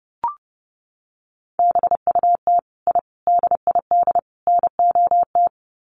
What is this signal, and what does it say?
Morse code, heard as a keyed tone, about 28 wpm, 700 Hz. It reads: BUT I DID NOT
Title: Sets of 4 Words 30wpm